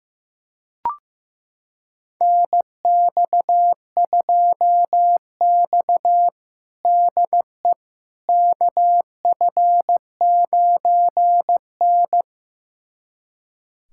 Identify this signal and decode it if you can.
Morse code, heard as a keyed tone, about 15 wpm, 700 Hz. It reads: NX2X DE KF9N
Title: USA Callsign DE Callsign - 4 Characters - 15wpm